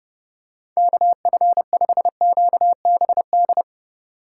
Morse code, heard as a keyed tone, about 30 wpm, 700 Hz. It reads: KF5QBD
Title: USA Call Signs - 6 Characters - 3x Character Spacing - 30wpm